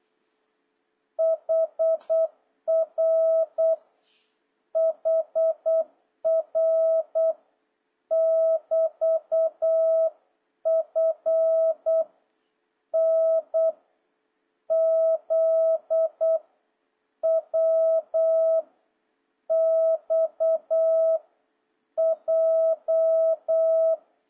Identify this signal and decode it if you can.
Morse code, heard as a keyed tone, about 8 wpm, 650 Hz. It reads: HR HR <BT> F N Z W X J
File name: jh8cbh / 2026cw